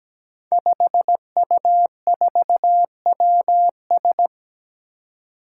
Morse code, heard as a keyed tone, about 17 wpm, 700 Hz. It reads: 5U4WS
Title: Five Challenging Letter-Number Combinations 17wpm